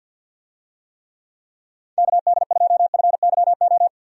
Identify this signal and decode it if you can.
Morse code, heard as a keyed tone, about 51 wpm, 700 Hz. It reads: KD2FYO
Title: USA Call Signs - 6 Characters - 50wpm